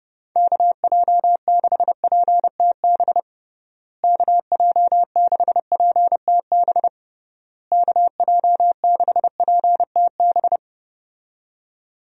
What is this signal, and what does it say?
Morse code, heard as a keyed tone, about 30 wpm, 700 Hz. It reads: KJ6PTB KJ6PTB KJ6PTB
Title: USA Call Signs - 6 Characters - Repeated 3x - 30wpm